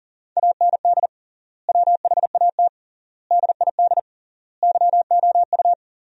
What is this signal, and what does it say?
Morse code, heard as a keyed tone, about 40 wpm, 700 Hz. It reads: AND WHAT DID YOU
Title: 3x Word-Spacing - Sentences from Top 100 Words - 40wpm